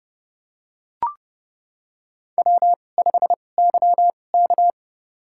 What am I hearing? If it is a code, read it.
Morse code, heard as a keyed tone, about 30 wpm, spaced as for 22 wpm, 700 Hz. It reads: W5YK
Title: USA Call Signs - 4 Characters - 2x Character Spacing - 30wpm